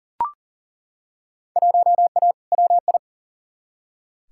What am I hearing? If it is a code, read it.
Morse code, heard as a keyed tone, about 40 wpm, 700 Hz. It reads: 1A WI
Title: Field Day Exchanges - 40wpm